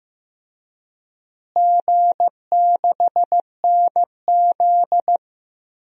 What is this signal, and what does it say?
Morse code, heard as a keyed tone, about 15 wpm, 700 Hz. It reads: G6NZ
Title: DXCC-100 Call Signs - 4 Characters - Repeated 2x - 15wpm